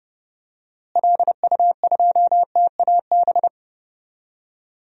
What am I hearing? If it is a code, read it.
Morse code, heard as a keyed tone, about 30 wpm, 700 Hz. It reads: LU2TAB
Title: DXCC-100 Call Signs - 6 Characters - Repeated 2x - 30wpm